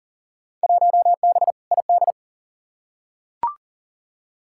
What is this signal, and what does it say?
Morse code, heard as a keyed tone, about 40 wpm, 700 Hz. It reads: 1B ID
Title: Field Day Exchanges - 40wpm